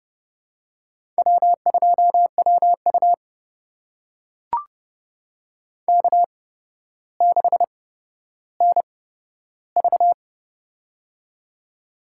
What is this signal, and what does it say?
Morse code, heard as a keyed tone, about 30 wpm, spaced as for 13 wpm, 700 Hz. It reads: W2WU K6NV
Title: USA Call Signs - 4 Characters - 8x Character Spacing - 30wpm